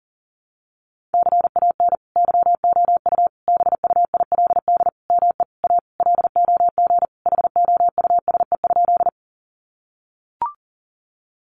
Morse code, heard as a keyed tone, about 40 wpm, 700 Hz. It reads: CAN YOU BUILD ME A LOG HOUSE?
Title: Sentences from Top 1400 Words 40wpm